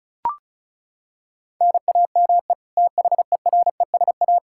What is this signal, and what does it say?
Morse code, heard as a keyed tone, about 36 wpm, 700 Hz. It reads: NAME THERESA
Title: K1USN SST QSO Practice 35wpm